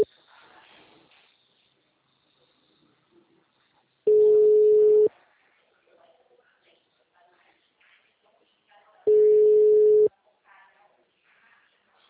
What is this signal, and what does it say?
Ringback tone (Europe, 425 Hz), with speech.
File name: out- -2035